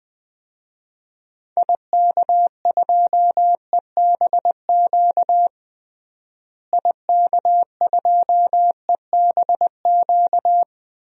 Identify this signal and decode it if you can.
Morse code, heard as a keyed tone, about 21 wpm, 700 Hz. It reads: IK2EBQ IK2EBQ
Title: DXCC-100 Call Signs - 6 Characters - Repeated 3x - 20wpm